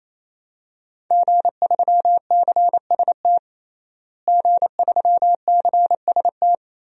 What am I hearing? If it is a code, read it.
Morse code, heard as a keyed tone, about 28 wpm, 700 Hz. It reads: G3CST G3CST
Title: DXCC-100 Call Signs - 5 Characters - Repeated 3x - 28wpm